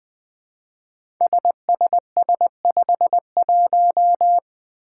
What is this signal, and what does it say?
Morse code, heard as a keyed tone, about 20 wpm, 700 Hz. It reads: SSS51
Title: Five Challenging Letter-Number Combinations 20wpm